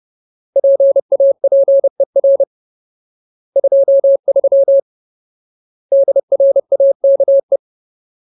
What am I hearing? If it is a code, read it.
Morse code, heard as a keyed tone, about 30 wpm, 550 Hz. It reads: PAPER 23 DRAKE
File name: Ultimate Word List / 30WPM Ultimate